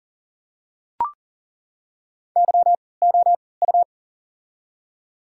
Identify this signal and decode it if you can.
Morse code, heard as a keyed tone, about 40 wpm, spaced as for 23 wpm, 700 Hz. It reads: YOU
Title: 3x Character Spacing - Top 300 Words - 40wpm